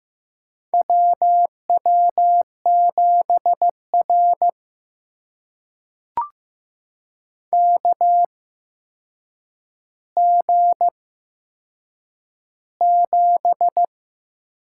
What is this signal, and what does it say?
Morse code, heard as a keyed tone, about 15 wpm, spaced as for 7 wpm, 700 Hz. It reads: WW7R KG7
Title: USA Call Signs - 8x Character Spacing - 15wpm